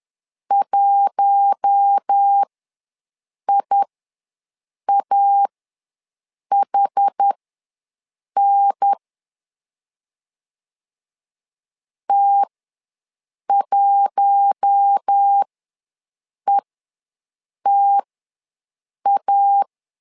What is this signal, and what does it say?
Morse code, heard as a keyed tone, about 10 wpm, spaced as for 6 wpm, 800 Hz. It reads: 1IAHN T1ETA
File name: LEZIONI / 4 Q1 ws 10wpm 2538